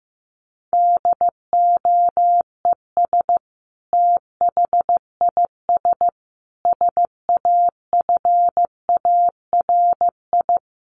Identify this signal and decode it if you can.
Morse code, heard as a keyed tone, about 15 wpm, 700 Hz. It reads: DOES THIS SAFARI
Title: A Sound Of Thunder 15wpm